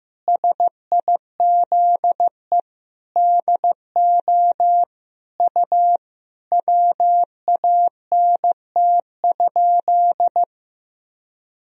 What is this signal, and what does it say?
Morse code, heard as a keyed tone, about 15 wpm, 700 Hz. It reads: SIZE DO U WANT?
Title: Sentences from Top 400 Words and Abbreviations 15wpm